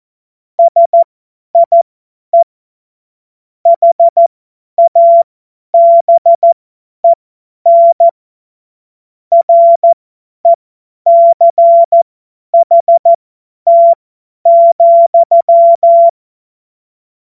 Morse code, heard as a keyed tone, about 14 wpm, spaced as for 10 wpm, 680 Hz. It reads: SIE HABEN RECHT,
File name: Morsekurs / Witze / Witz 05. April 2022 50 F70